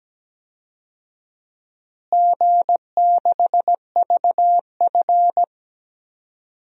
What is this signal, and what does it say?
Morse code, heard as a keyed tone, about 17 wpm, 700 Hz. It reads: G6VF
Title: DXCC-100 Call Signs - 4 Characters - Repeated 3x - 17wpm